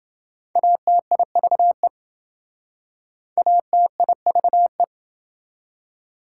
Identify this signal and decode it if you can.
Morse code, heard as a keyed tone, about 27 wpm, 700 Hz. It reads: ATIVE ATIVE